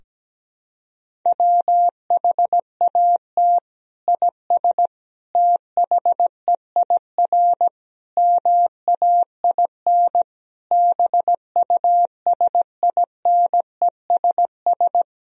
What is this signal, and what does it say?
Morse code, heard as a keyed tone, about 17 wpm, 700 Hz. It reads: WHAT IS THEIR MAIN BUSINESS
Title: Sentences from Top 1000 Words - Review Part 2 - 17wpm